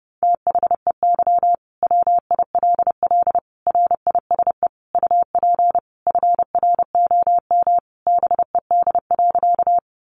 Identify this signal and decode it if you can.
Morse code, heard as a keyed tone, about 30 wpm, 700 Hz. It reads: THEY WILL RISE UP FROM BED.
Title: Sentences from Top 900 Words 30wpm